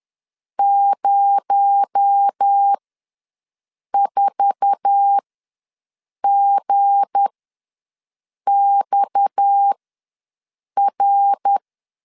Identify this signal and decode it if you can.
Morse code, heard as a keyed tone, about 10 wpm, spaced as for 6 wpm, 800 Hz. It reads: 04GXR
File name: LEZIONI / 8 YJ45 / 8 YJ4 ws 10wpm 4020